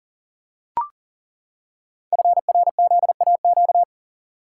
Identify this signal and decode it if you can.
Morse code, heard as a keyed tone, about 42 wpm, 700 Hz. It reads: FRZAQ
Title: Code Groups - 5 Letters - 40wpm